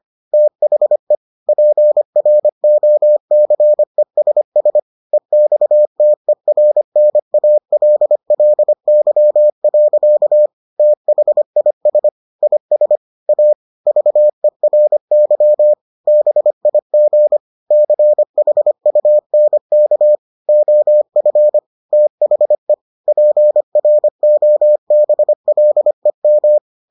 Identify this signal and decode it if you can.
Morse code, heard as a keyed tone, about 25 wpm, 600 Hz. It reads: THE PROCESS EXTERNALLY. THIS IS A VERY BIG CHUNK OF THE PROBLEM